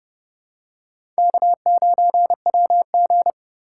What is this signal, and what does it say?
Morse code, heard as a keyed tone, about 30 wpm, 700 Hz. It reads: K9WG
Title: USA Call Signs - 2x Character Spacing - 30wpm